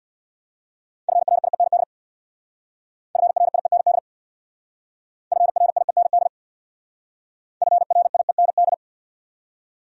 Morse code, heard as a keyed tone, about 66 wpm, 700 Hz. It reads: FRIEND FRIEND FRIEND FRIEND